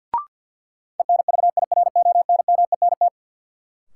Morse code, heard as a keyed tone, about 52 wpm, 700 Hz. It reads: ENVIRONMENT
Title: Sentences from Top 1000 Words - Review Part 5 - New Word Drill - 50wpm